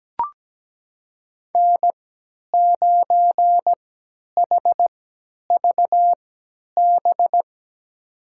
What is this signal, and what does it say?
Morse code, heard as a keyed tone, about 17 wpm, spaced as for 10 wpm, 700 Hz. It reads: N9HVB
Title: USA Call Signs - 5 Characters - 3x Character Spacing - 17wpm